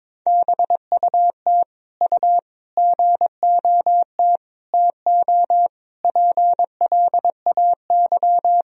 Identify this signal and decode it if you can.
Morse code, heard as a keyed tone, about 22 wpm, 700 Hz. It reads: BUT U GOT TO PLAY
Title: Sentences from Top 300 Words and Abbreviations 22wpm